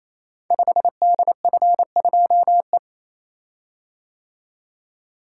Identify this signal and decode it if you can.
Morse code, heard as a keyed tone, about 27 wpm, 700 Hz. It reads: 5DF2E
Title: Code Groups - 5 Letter:Numbers - 28wpm